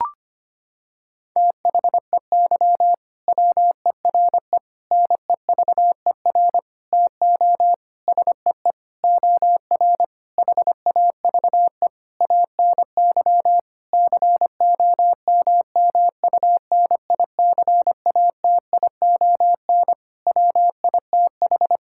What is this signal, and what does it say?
Morse code, heard as a keyed tone, about 25 wpm, 700 Hz. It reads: THEY WERE NEVER TO SEE OR HAVE ANY COMMUNICATION WITH
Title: Sentences from Top 1000 Words - Review Part 6 - 25wpm